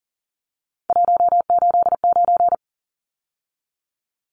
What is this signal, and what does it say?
Morse code, heard as a keyed tone, about 40 wpm, 700 Hz. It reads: 189
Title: Numbers - 3-Digit 40wpm